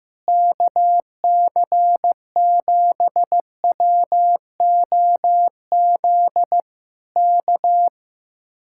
Morse code, heard as a keyed tone, about 15 wpm, 700 Hz. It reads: KC7WOZ K
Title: Speed-Racing Common QSO Phrases Encore - 15wpm